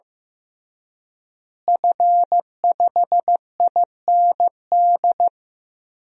Morse code, heard as a keyed tone, about 15 wpm, 700 Hz. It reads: F5IND
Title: DXCC-100 Call Signs - 5 Characters - Repeated 2x - 15wpm